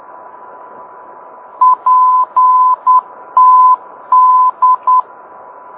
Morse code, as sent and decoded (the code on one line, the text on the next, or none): .--. - -..
PTD